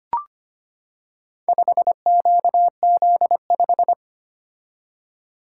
Morse code, heard as a keyed tone, about 25 wpm, 700 Hz. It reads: 5QZ5
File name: practice / 25wpm / Four Letter-Number Combinations - 25wpm